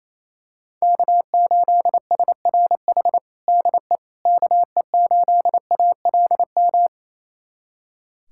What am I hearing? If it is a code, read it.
Morse code, heard as a keyed tone, about 28 wpm, 700 Hz. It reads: K8SRH DE KE8ALM
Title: USA Callsign DE Callsign 28wpm